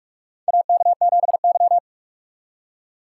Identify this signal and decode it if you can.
Morse code, heard as a keyed tone, about 47 wpm, 700 Hz. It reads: AK7Y
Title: USA Call Signs - 3x Character Spacing - 45wpm